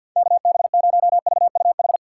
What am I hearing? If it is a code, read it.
Morse code, heard as a keyed tone, about 53 wpm, 700 Hz. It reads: KB0VUH
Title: USA Call Signs - 6 Characters - 8x Character Spacing - 50wpm